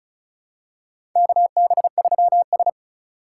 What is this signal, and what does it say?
Morse code, heard as a keyed tone, about 36 wpm, 690 Hz. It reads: KB3S
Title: USA Call Signs - 4 Characters - 3x Character Spacing - 35wpm